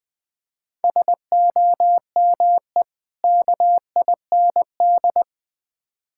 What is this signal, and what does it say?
Morse code, heard as a keyed tone, about 20 wpm, 700 Hz. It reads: SOME KIND
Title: Build A Phrase - 3-Word Phrases - 20wpm